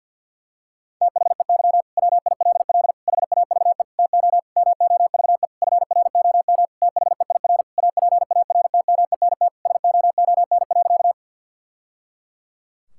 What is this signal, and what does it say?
Morse code, heard as a keyed tone, about 50 wpm, 700 Hz. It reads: THEY WILL HAVE TO MOVE FROM THEIR APARTMENT SOON.
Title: Sentences from Top 1000 Words - Review Part 6 - 50wpm